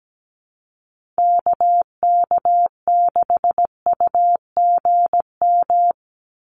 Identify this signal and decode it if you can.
Morse code, heard as a keyed tone, about 17 wpm, 700 Hz. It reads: KK6UGM
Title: USA Call Signs 17wpm